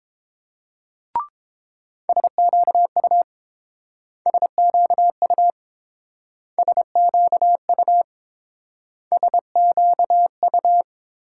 Morse code, heard as a keyed tone, about 30 wpm, 700 Hz. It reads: SQU SQU SQU SQU